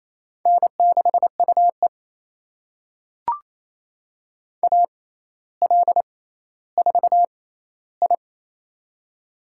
Morse code, heard as a keyed tone, about 28 wpm, spaced as for 14 wpm, 700 Hz. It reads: N6UE AL4I
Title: USA Call Signs - 4 Characters - 6x Character Spacing - 28wpm